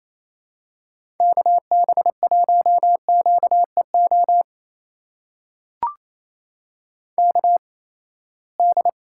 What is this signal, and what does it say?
Morse code, heard as a keyed tone, about 28 wpm, spaced as for 19 wpm, 700 Hz. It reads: KB1QEO KD
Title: USA Call Signs - 6 Characters - 8x Character Spacing - 28wpm